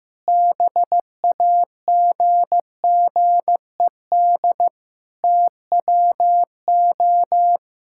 Morse code, heard as a keyed tone, about 15 wpm, 700 Hz. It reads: BAGGED TWO
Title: Pangram Challenge v2 - 15wpm